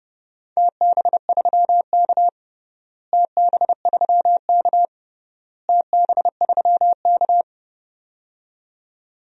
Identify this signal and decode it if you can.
Morse code, heard as a keyed tone, about 30 wpm, 700 Hz. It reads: TB3K TB3K TB3K
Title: DXCC-100 Call Signs - 4 Characters - Repeated 3x - 30wpm